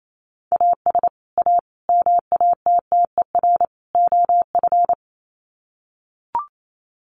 Morse code, heard as a keyed tone, about 28 wpm, 700 Hz. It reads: AS A MATTER OF
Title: Sets of 4 Words 28wpm